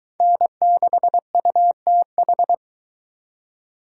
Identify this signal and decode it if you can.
Morse code, heard as a keyed tone, about 22 wpm, 700 Hz. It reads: N6UTH